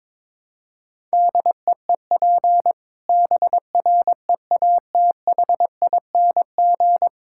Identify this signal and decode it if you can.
Morse code, heard as a keyed tone, about 22 wpm, 700 Hz. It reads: DEEP BREATHING
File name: Difficult Sentences with Hints - 22wpm